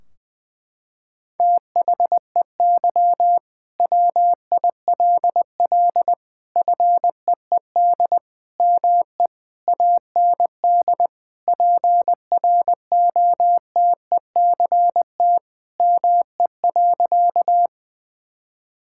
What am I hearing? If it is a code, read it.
Morse code, heard as a keyed tone, about 20 wpm, 700 Hz. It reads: THEY WILL FEED ME AND PROTECT ME.
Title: Sentences from Top 1600 Words 20wpm